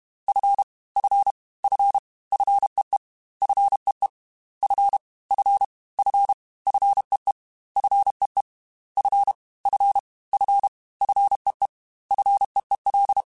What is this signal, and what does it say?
Morse code, heard as a keyed tone, about 32 wpm, 800 Hz. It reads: F F F FEE FEE F F F FEE FEE F F F FEE FEEL
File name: Fs / Single F mp3s / Single F Prefixes 3x2 Twice @30